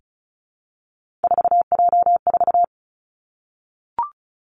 Morse code, heard as a keyed tone, about 36 wpm, 700 Hz. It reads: 4J4
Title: Three Challenging Letter-Number Combinations 35wpm